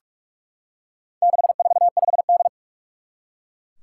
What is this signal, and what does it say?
Morse code, heard as a keyed tone, about 44 wpm, 700 Hz. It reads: 645D
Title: Four Challenging Letter-Number Combinations 45wpm